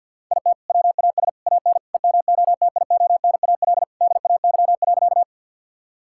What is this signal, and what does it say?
Morse code, heard as a keyed tone, about 48 wpm, 700 Hz. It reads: IT WAS AN EMOTIONAL DAY.
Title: Sentences from Top 1000 Words - Review Part 5 - 50wpm